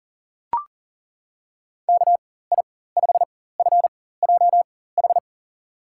Morse code, heard as a keyed tone, about 40 wpm, spaced as for 19 wpm, 700 Hz. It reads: KI5FJH
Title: USA Call Signs - 6 Characters - 4x Character Spacing - 40wpm